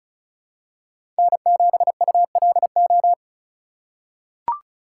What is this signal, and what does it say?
Morse code, heard as a keyed tone, about 36 wpm, 700 Hz. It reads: N7ULO
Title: USA Call Signs - 5 Characters - 4x Character Spacing - 35wpm